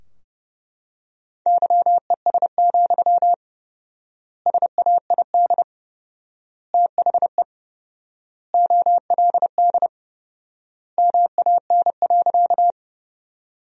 Morse code, heard as a keyed tone, about 30 wpm, 700 Hz. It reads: YES, SAID THE OLD MAN.
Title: 4x Word-Spacing - Sentences from Top 500 Words - 30wpm